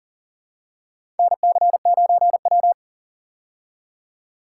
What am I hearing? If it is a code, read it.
Morse code, heard as a keyed tone, about 40 wpm, 700 Hz. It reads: NC9W
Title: K1USN SST QSO Practice - 40wpm